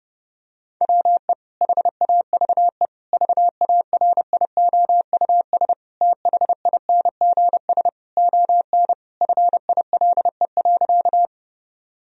Morse code, heard as a keyed tone, about 30 wpm, 700 Hz. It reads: WE HAVE VARIOUS THINGS ON FILE.
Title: Sentences from Top 1000 Words - Review Part 4 - 30wpm